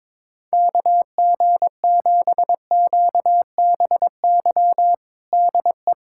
Morse code, heard as a keyed tone, about 22 wpm, 700 Hz. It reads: KG7QBY DE
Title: USA Callsign DE Callsign - 6 Characters - 22wpm